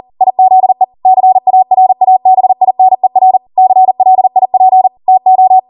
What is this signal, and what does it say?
Morse code, heard as a keyed tone, about 40 wpm, 750 Hz. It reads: IZE CARABINER CLIP TO
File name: Morse / 40WPM